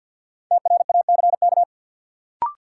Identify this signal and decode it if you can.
Morse code, heard as a keyed tone, about 48 wpm, 700 Hz. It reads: TRACK
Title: Sentences from Top 1100 Words - New Word Drill Alt - 50wpm